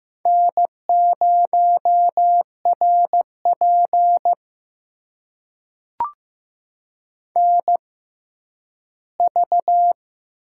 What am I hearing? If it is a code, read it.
Morse code, heard as a keyed tone, about 15 wpm, spaced as for 10 wpm, 700 Hz. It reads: N0RP NV
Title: USA Call Signs - 4 Characters - 6x Character Spacing - 15wpm